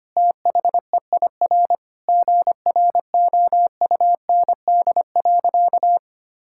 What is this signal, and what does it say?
Morse code, heard as a keyed tone, about 25 wpm, 700 Hz. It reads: THEIR GROUND.
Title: Sentences from Top 1000 Words - Review Part 5 - 25wpm